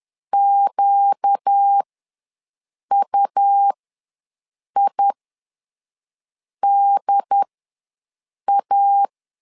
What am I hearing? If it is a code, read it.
Morse code, heard as a keyed tone, about 10 wpm, spaced as for 5 wpm, 800 Hz. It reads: QUIDA
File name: LEZIONI / 4 Q ws 10wpm 2306